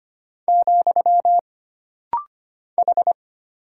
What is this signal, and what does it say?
Morse code, heard as a keyed tone, about 25 wpm, 700 Hz. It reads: , H
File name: Single Character (Letter:Number:Punctuation:Prosign) - Rapid-Fire - 25wpm